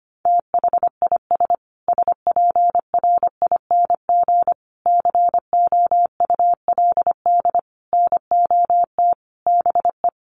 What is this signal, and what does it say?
Morse code, heard as a keyed tone, about 25 wpm, 700 Hz. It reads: THIS SPRING COULD NOT BE
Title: Sentences from Top 1500 Words 25wpm